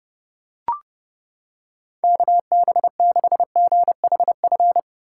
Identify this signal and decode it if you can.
Morse code, heard as a keyed tone, about 29 wpm, 700 Hz. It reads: KB6GHF